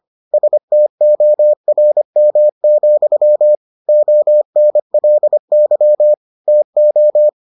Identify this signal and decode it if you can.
Morse code, heard as a keyed tone, about 25 wpm, 600 Hz. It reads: STORM, ONLY TO